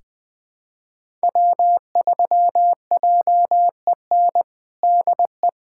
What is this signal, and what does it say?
Morse code, heard as a keyed tone, about 20 wpm, 700 Hz. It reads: W3JEN DE
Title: USA Callsign DE Callsign - 5 Characters - 20wpm